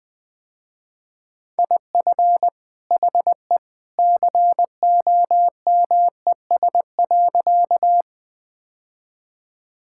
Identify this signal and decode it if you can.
Morse code, heard as a keyed tone, about 21 wpm, 700 Hz. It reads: IF HE COMES.
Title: Sentences with 2-Word Phrases - Spoken Beginning - 20wpm